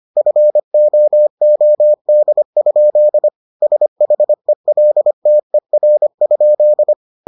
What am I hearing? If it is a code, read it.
Morse code, heard as a keyed tone, about 25 wpm, 600 Hz. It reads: FOOD? SHELTER?